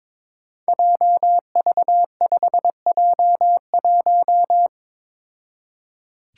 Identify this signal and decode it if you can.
Morse code, heard as a keyed tone, about 22 wpm, 700 Hz. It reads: JV5J1
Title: Five Challenging Letter-Number Combinations 22wpm